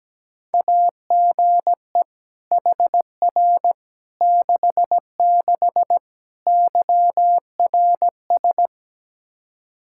Morse code, heard as a keyed tone, about 17 wpm, 700 Hz. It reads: AGE HR 66 YRS
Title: Practice QSO - Exchange 3 - More Info - 17wpm